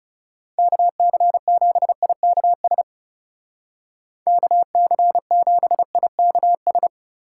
Morse code, heard as a keyed tone, about 36 wpm, 700 Hz. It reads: KC7IKS KC7IKS
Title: Speed Racing - USA 2x3 Call Signs 30wpm